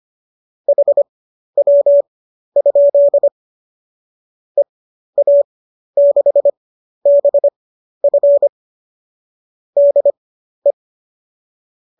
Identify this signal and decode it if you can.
Morse code, heard as a keyed tone, about 25 wpm, spaced as for 12 wpm, 570 Hz. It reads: HW? EA6BF DE
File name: qso009 12